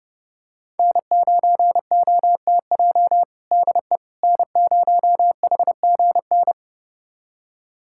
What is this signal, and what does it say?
Morse code, heard as a keyed tone, about 30 wpm, 700 Hz. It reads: N9OTJ DE N0HGN
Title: USA Callsign DE Callsign - 5 Characters - 30wpm